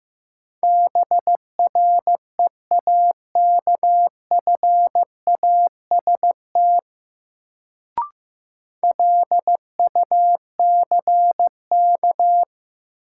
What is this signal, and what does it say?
Morse code, heard as a keyed tone, about 15 wpm, 700 Hz. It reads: BREAKFAST LUCK
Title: Sentences from Top 1500 Words - New Word Drill - 15wpm